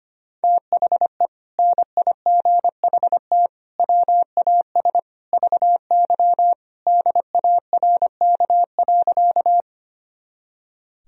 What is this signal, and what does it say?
Morse code, heard as a keyed tone, about 25 wpm, 700 Hz. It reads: THE NIGHT WAS VY DARK.
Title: Sentences from Top 500 Words and Abbreviations 25wpm